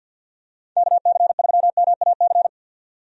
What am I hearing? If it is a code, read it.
Morse code, heard as a keyed tone, about 51 wpm, 700 Hz. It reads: KC3MAC
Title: USA Call Signs - 6 Characters - 50wpm